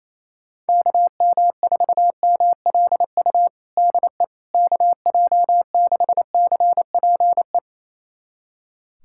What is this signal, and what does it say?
Morse code, heard as a keyed tone, about 28 wpm, 700 Hz. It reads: KM4MLU DE KJ6CPE
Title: USA Callsign DE Callsign 28wpm